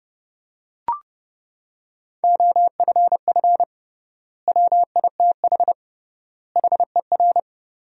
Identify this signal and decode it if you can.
Morse code, heard as a keyed tone, about 30 wpm, 700 Hz. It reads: OFF WITH HER
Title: 3x Word-Spacing - Sentences from Top 300 Words - 30wpm